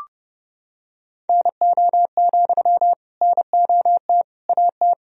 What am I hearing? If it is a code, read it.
Morse code, heard as a keyed tone, about 30 wpm, 700 Hz. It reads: NO, NOT AT
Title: Sentences from Top 1000 Words - Review Part 4 - 30wpm